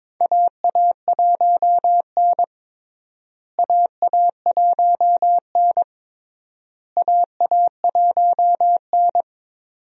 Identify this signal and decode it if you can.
Morse code, heard as a keyed tone, about 22 wpm, 700 Hz. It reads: AA1N AA1N AA1N
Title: USA Call Signs - 4 Characters - Repeated 3x - 22wpm